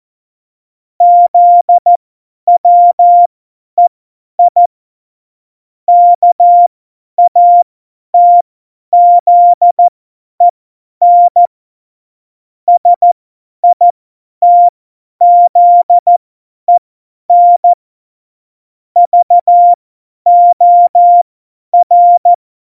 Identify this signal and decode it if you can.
Morse code, heard as a keyed tone, about 14 wpm, spaced as for 10 wpm, 700 Hz. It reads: ZWEI KATZEN SITZEN VOR
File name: Morsekurs / Witze / Witz 04. September 2018, 50 F70